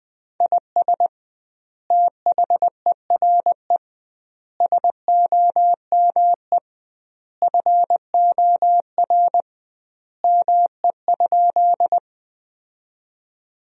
Morse code, heard as a keyed tone, about 21 wpm, 700 Hz. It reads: IS THERE SOME FOR ME?
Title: 2x Word-Spacing - Sentences from Top 200 Words - 20wpm